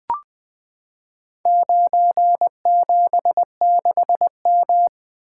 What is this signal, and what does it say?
Morse code, heard as a keyed tone, about 20 wpm, 700 Hz. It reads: 976M
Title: Four Letter-Number Combinations - 20wpm